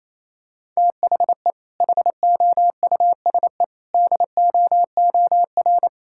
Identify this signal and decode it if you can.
Morse code, heard as a keyed tone, about 28 wpm, 700 Hz. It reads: THE HOUSE DOOR
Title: Sentences from Top 1000 Words - Review Part 6 - 28wpm